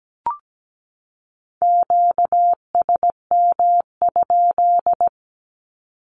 Morse code, heard as a keyed tone, about 17 wpm, 700 Hz. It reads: QSM?
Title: Q-Codes 17wpm